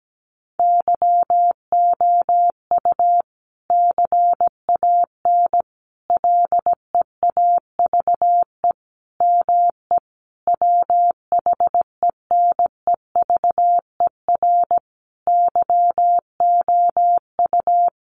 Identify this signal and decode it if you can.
Morse code, heard as a keyed tone, about 17 wpm, 700 Hz. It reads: YOU CAN LEAVE ME WHENEVER YOU
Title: Sentences from Top 1200 Words 17wpm